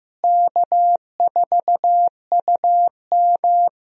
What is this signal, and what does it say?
Morse code, heard as a keyed tone, about 15 wpm, 700 Hz. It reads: K4UM
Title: USA Call Signs - 4 Characters - Repeated 3x - 15wpm